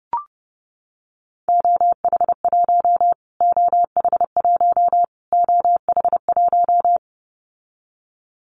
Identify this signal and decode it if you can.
Morse code, heard as a keyed tone, about 28 wpm, 700 Hz. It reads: OH1 OH1 OH1
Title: Callsign Prefixes for Top 25 DXCC - Repeated 3x - 30wpm